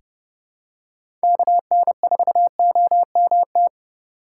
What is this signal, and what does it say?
Morse code, heard as a keyed tone, about 30 wpm, 700 Hz. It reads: KN4OMT
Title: USA Call Signs - 6 Characters - 3x Character Spacing - 30wpm